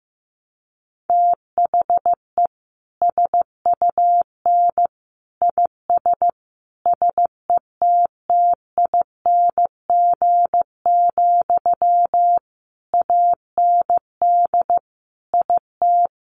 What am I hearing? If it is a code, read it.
Morse code, heard as a keyed tone, about 15 wpm, 700 Hz. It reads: THE SUN IS SETTING, AND IT
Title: Sentences from Top 1300 Words 15wpm